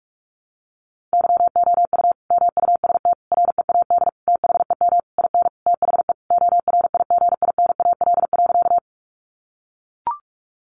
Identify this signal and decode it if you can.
Morse code, heard as a keyed tone, about 44 wpm, 700 Hz. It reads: YOU MUST READ THEM IN THE ORIGINAL.
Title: Sentences from Top 900 Words 45wpm